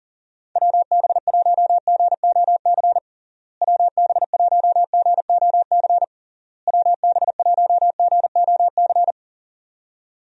Morse code, heard as a keyed tone, about 37 wpm, 700 Hz. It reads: WB1GOC WB1GOC WB1GOC
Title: USA Call Signs - 6 Characters - Repeated 3x - 40wpm